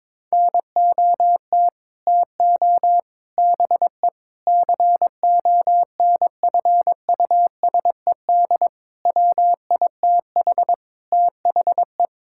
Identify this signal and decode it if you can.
Morse code, heard as a keyed tone, about 22 wpm, 700 Hz. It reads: NOT TO BE CONFUSED WITH THE
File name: practice / 22wpm / Sets of 6 Words - 22wpm